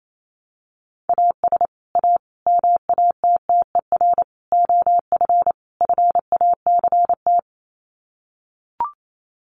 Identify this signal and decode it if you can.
Morse code, heard as a keyed tone, about 28 wpm, 700 Hz. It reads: AS A MATTER OF FACT
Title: Sets of 5 Words 28wpm